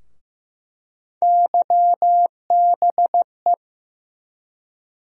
Morse code, heard as a keyed tone, about 15 wpm, 700 Hz. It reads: YBE
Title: USA Call Sign Postfixes - 3 Letters 15wpm